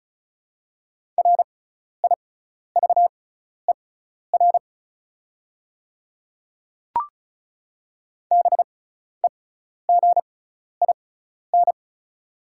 Morse code, heard as a keyed tone, about 36 wpm, spaced as for 12 wpm, 690 Hz. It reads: RIVER BEGIN
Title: 6x Character Spacing - Top 300 Words - 35wpm